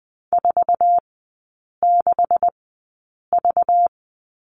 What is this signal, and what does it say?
Morse code, heard as a keyed tone, about 20 wpm, spaced as for 8 wpm, 690 Hz. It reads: 46V
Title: 212 Lesson 34e - Number 8 - Three Characters 20wpm